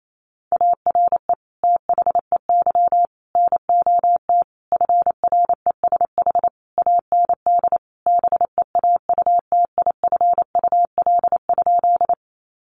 Morse code, heard as a keyed tone, about 28 wpm, 700 Hz. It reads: ARE THEY NOT FRESH AND BEAUTIFUL?
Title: Sentences from Top 1200 Words 28wpm